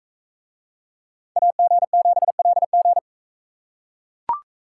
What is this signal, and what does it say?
Morse code, heard as a keyed tone, about 40 wpm, 700 Hz. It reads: AG7LG